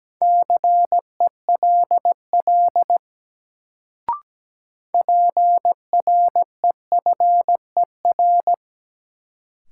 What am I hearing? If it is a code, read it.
Morse code, heard as a keyed tone, about 17 wpm, 700 Hz. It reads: CELL PREFER
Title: Sentences from Top 1100 Words - New Word Drill - 17wpm